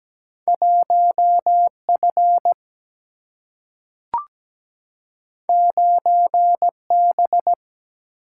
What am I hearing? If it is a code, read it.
Morse code, heard as a keyed tone, about 17 wpm, 700 Hz. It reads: 1 F 9 B
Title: Single-Letter Single-Number 17wpm